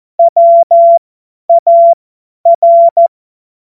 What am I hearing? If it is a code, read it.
Morse code, heard as a keyed tone, about 14 wpm, spaced as for 10 wpm, 680 Hz. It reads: WAR
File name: Morsekurs / Witze / Witz 14. Februar 2023 50 F70